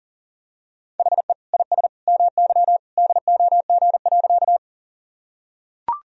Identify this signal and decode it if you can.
Morse code, heard as a keyed tone, about 40 wpm, 700 Hz. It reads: HE IS MY DOG.
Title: Sentences from Top 1000 Words - Review Part 2 - 40wpm